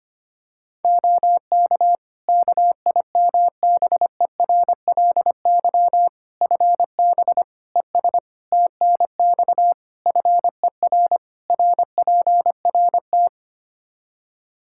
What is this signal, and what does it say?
Morse code, heard as a keyed tone, about 25 wpm, 700 Hz. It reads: OK KIMBERLY FB ES TNX FER RPRT
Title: Practice QSO - Exchange 2 - Info - 25wpm